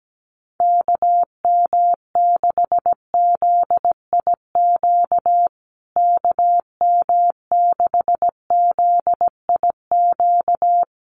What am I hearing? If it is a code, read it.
Morse code, heard as a keyed tone, about 17 wpm, 700 Hz. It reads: KM6ZIQ KM6ZIQ
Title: USA Call Signs Repeated 2x 17wpm